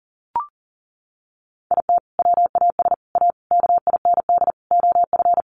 Morse code, heard as a keyed tone, about 40 wpm, 700 Hz. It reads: IT WAS A KIND OF
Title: Sentences from Top 500 Words 40wpm